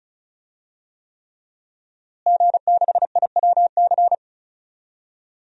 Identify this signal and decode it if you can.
Morse code, heard as a keyed tone, about 36 wpm, 700 Hz. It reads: G6IWC
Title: DXCC-100 Call Signs - 5 Characters - Repeated 3x - 35wpm